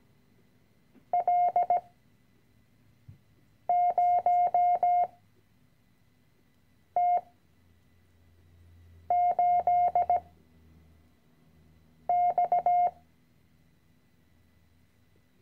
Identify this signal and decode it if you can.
Morse code, heard as a keyed tone, about 17 wpm, 710 Hz. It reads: L 0 T 8 X
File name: CW-Online-Kurs / Anfänger / Lektion 010